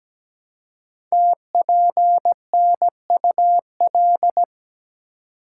Morse code, heard as a keyed tone, about 17 wpm, 700 Hz. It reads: TPNUL
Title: Code Groups - 5 Letters - 17wpm